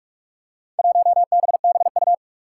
Morse code, heard as a keyed tone, about 44 wpm, 700 Hz. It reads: 1BBU
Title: Four Challenging Letter-Number Combinations 45wpm